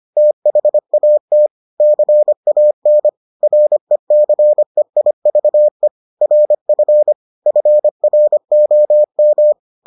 Morse code, heard as a keyed tone, about 25 wpm, 600 Hz. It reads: THAT CAN RECEIVE RF FROM